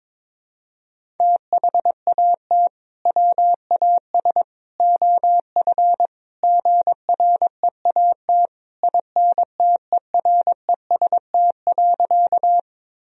Morse code, heard as a keyed tone, about 22 wpm, 700 Hz. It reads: THAT WAS OF GREAT INTEREST.
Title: Sentences from Top 1000 Words - Review Part 2 - 22wpm